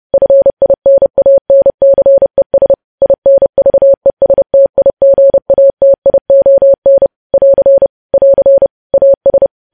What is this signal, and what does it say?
Morse code, heard as a keyed tone, about 30 wpm, 550 Hz. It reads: FINANCES INVESTIGATION <AR> <AR> AS
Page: news headlines in morse code at 30 wpm